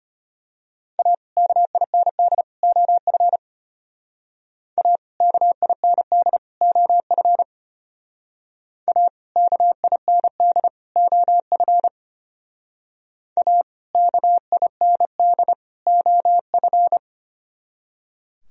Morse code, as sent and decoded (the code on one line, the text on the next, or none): .- / -.- .. -. -.. / --- ..-. / .- / -.- .. -. -.. / --- ..-. / .- / -.- .. -. -.. / --- ..-. / .- / -.- .. -. -.. / --- ..-.
A KIND OF A KIND OF A KIND OF A KIND OF